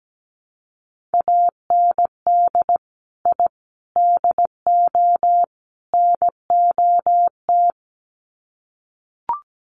Morse code, heard as a keyed tone, about 17 wpm, 700 Hz. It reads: AND I DO NOT
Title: Sets of 4 Words 17wpm